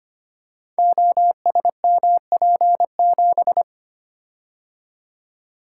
Morse code, heard as a keyed tone, about 25 wpm, 700 Hz. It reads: OSMP7
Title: Code Groups - 5 Letter:Numbers - 25wpm